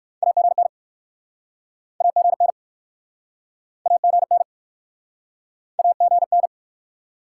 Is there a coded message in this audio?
Yes: Morse code from 0.2 to 6.5 s, heard as a keyed tone, about 61 wpm, 700 Hz. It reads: AGN AGN AGN AGN